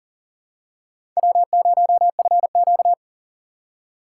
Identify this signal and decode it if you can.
Morse code, heard as a keyed tone, about 40 wpm, 700 Hz. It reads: W0FQ
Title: USA Call Signs - 4 Characters - 6x Character Spacing - 40wpm